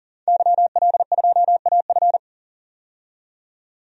Morse code, heard as a keyed tone, about 40 wpm, 700 Hz. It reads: YL2AF
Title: DXCC-100 Call Signs - 5 Characters - 40wpm